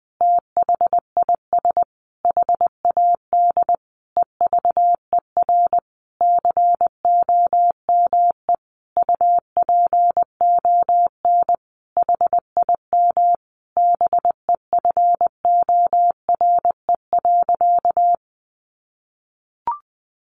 Morse code, heard as a keyed tone, about 20 wpm, 700 Hz. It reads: THIS HAD EVER COME UPON HIM BEFORE.
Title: Sentences from Top 1100 Words 20wpm